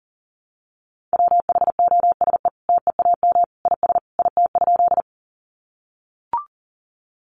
Morse code, heard as a keyed tone, about 40 wpm, 700 Hz. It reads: WHOSE TEAM IS IT?
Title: Sentences from Top 1500 Words 40wpm